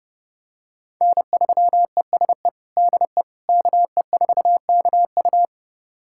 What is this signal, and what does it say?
Morse code, heard as a keyed tone, about 29 wpm, 700 Hz. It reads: N3ESE DE KE4KU
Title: USA Callsign DE Callsign - 5 Characters - 30wpm